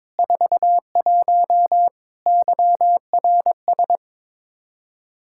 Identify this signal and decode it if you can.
Morse code, heard as a keyed tone, about 21 wpm, 700 Hz. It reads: 41 YRS
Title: Practice QSO - Exchange 3 - More Info - 22wpm